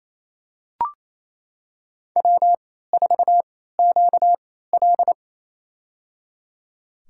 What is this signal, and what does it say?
Morse code, heard as a keyed tone, about 28 wpm, spaced as for 16 wpm, 700 Hz. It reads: W4QL
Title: USA Call Signs - 4 Characters - 3x Character Spacing - 28wpm